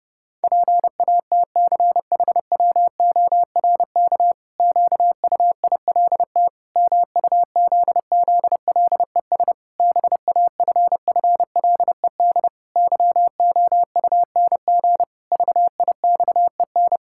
Morse code, heard as a keyed tone, about 30 wpm, 700 Hz. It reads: PATCHWORK QUILT MUZZLES BAFFLED YOUNG VIXEN
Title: Pangram Challenge v2 - 30wpm